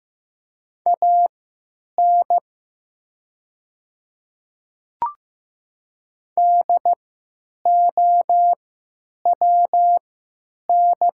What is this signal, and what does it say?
Morse code, heard as a keyed tone, about 15 wpm, spaced as for 9 wpm, 700 Hz. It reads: AN DOWN
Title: 3x Character Spacing - Top 100 Words - 15wpm